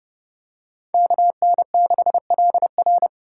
Morse code, heard as a keyed tone, about 29 wpm, 700 Hz. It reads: KN6LR